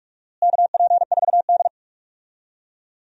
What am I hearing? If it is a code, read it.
Morse code, heard as a keyed tone, about 46 wpm, 700 Hz. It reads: KP4D
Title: DXCC-100 Call Signs - 4 Characters - Repeated 2x - 45wpm